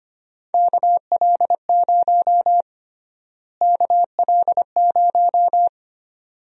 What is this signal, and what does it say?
Morse code, heard as a keyed tone, about 25 wpm, 700 Hz. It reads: KL0 KL0
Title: USA Call Sign Prefixes - Repeated 2x - 25wpm